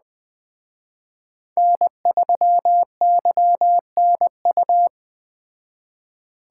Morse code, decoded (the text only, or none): N3YNU